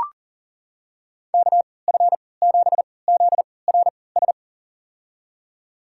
Morse code, heard as a keyed tone, about 40 wpm, spaced as for 23 wpm, 700 Hz. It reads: KF7ZRS
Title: USA Call Signs - 6 Characters - 3x Character Spacing - 40wpm